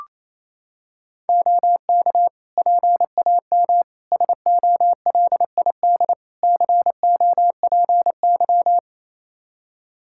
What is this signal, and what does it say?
Morse code, heard as a keyed tone, about 28 wpm, 700 Hz. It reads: OK PAM SOLID COPY
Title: Practice QSO - Exchange 3 - More Info - 28wpm